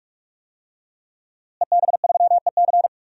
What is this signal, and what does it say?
Morse code, heard as a keyed tone, about 46 wpm, 700 Hz. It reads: EB3EC
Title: DXCC-100 Call Signs - 5 Characters - Repeated 2x - 45wpm